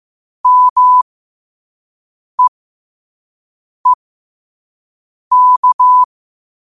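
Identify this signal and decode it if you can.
Morse code, heard as a keyed tone, about 15 wpm, spaced as for 5 wpm, 1000 Hz. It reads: MEEK
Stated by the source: Morse code quote of the day at 5 wpm.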